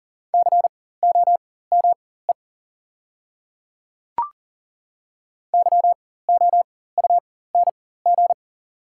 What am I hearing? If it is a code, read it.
Morse code, heard as a keyed tone, about 40 wpm, spaced as for 19 wpm, 690 Hz. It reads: COME YOUNG
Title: 4x Character Spacing - Top 500 Words - 40wpm